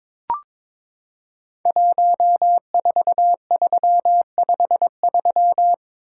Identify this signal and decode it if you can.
Morse code, heard as a keyed tone, about 22 wpm, 700 Hz. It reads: 14353
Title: Code Groups - 5 Letter:Numbers - 22wpm